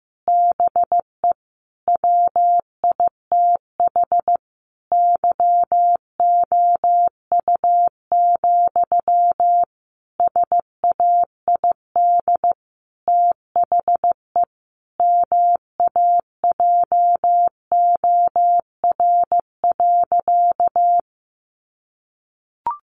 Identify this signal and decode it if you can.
Morse code, heard as a keyed tone, about 15 wpm, 700 Hz. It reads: BE WITH YOU, SAID THE MAJOR.
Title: Sentences from Top 600 Words 15wpm